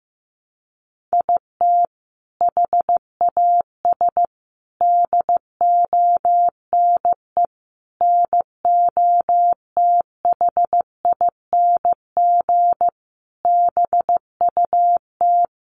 Morse code, heard as a keyed tone, about 15 wpm, 700 Hz. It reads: IT HAS DONE NOTHING BUT
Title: Sentences from Top 1100 Words 15wpm